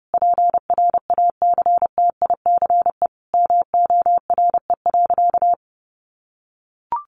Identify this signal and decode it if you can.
Morse code, heard as a keyed tone, about 30 wpm, 700 Hz. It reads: PRACTICE MORE.
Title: Sentences from Top 700 Words 30wpm